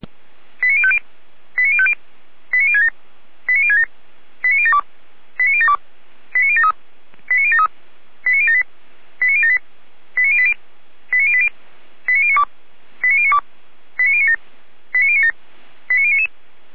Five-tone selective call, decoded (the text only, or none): 89050 89050 89067 89067 89072 89072 89073 89073 89078 89078 89080 89080 89092 89092 89097 89097 89099